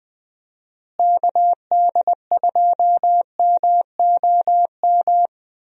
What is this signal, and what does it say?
Morse code, heard as a keyed tone, about 21 wpm, 700 Hz. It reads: KD2MOM
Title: USA Call Signs - 6 Characters - Repeated 3x - 20wpm